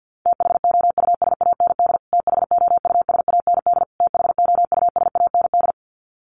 Morse code, heard as a keyed tone, about 49 wpm, 700 Hz. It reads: THOUSAND THOUSAND THOUSAND
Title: Top 500 Words Repeated 50wpm